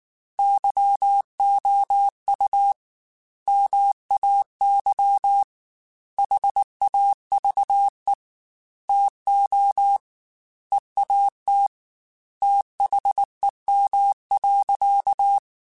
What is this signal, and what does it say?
Morse code, heard as a keyed tone, about 19 wpm, 800 Hz. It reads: YOU MAY HAVE TO EAT THEM.
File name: Chuckles / Chuckles-mp3s / SoftWords @18